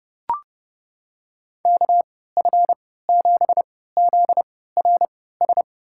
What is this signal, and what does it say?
Morse code, heard as a keyed tone, about 32 wpm, spaced as for 17 wpm, 700 Hz. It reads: KF7ZRS
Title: USA Call Signs - 6 Characters - 3x Character Spacing - 30wpm